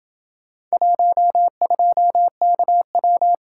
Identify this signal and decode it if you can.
Morse code, heard as a keyed tone, about 28 wpm, 700 Hz. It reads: 12KW